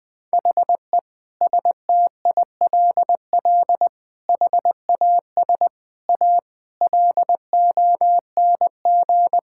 Morse code, heard as a keyed tone, about 20 wpm, 700 Hz. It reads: HE STILL HAS A LONG
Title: Sentences with 7-Word Phrases - 20wpm